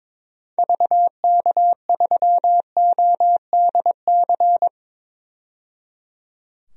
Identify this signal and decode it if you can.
Morse code, heard as a keyed tone, about 21 wpm, 700 Hz. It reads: VK3ODC
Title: DXCC-100 Call Signs - 6 Characters - 22wpm